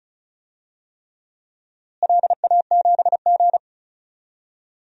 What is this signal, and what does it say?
Morse code, heard as a keyed tone, about 36 wpm, 700 Hz. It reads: LA7G
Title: DXCC-100 Call Signs - 4 Characters - Repeated 3x - 35wpm